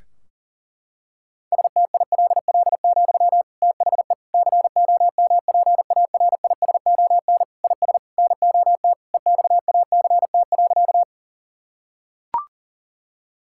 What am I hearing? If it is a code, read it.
Morse code, heard as a keyed tone, about 40 wpm, 700 Hz. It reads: STILL, THE COMPARISON IS NOT EXACT.
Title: Sentences from Top 1600 Words 40wpm